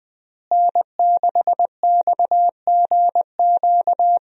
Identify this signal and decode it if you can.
Morse code, heard as a keyed tone, about 20 wpm, 700 Hz. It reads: N6XGQ
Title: USA Call Signs - 5 Characters - Repeated 3x - 20wpm